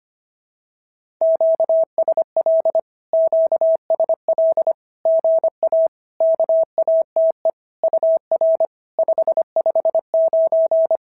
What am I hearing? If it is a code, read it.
Morse code, heard as a keyed tone, about 25 wpm, 650 Hz. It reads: QSL QSL GA KATE UR 559
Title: SOTA QSO Practice 25wpm